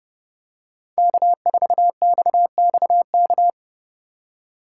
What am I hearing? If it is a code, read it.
Morse code, heard as a keyed tone, about 30 wpm, 700 Hz. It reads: K4XXK